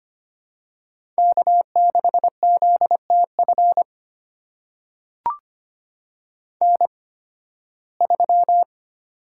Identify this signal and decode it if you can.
Morse code, heard as a keyed tone, about 25 wpm, spaced as for 16 wpm, 700 Hz. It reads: K6ZTF N3
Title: USA Call Signs - 5 Characters - 8x Character Spacing - 25wpm